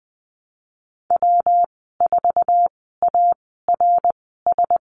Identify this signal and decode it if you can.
Morse code, heard as a keyed tone, about 20 wpm, spaced as for 15 wpm, 700 Hz. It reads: W4ARS
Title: 167 Lesson 27g - Number 7 - Call Signs 20wpm